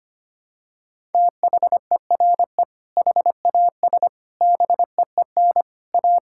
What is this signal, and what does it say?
Morse code, heard as a keyed tone, about 25 wpm, 700 Hz. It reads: THERE HAS BEEN A
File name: practice / 25wpm / Sentences with 3-Word Phrases - 25wpm